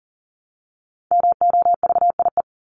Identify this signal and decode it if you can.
Morse code, heard as a keyed tone, about 40 wpm, 700 Hz. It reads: MOVIE
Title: Top 501-750 Words 40wpm